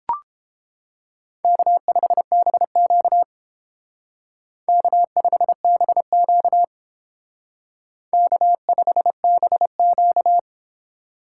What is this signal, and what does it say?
Morse code, heard as a keyed tone, about 29 wpm, 700 Hz. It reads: K5BQ K5BQ K5BQ